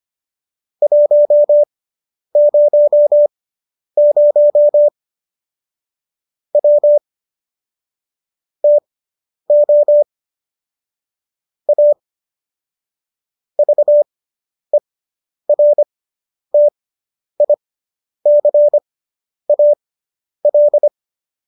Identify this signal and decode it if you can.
Morse code, heard as a keyed tone, about 25 wpm, spaced as for 10 wpm, 590 Hz. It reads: 100 W TO A VERTICAL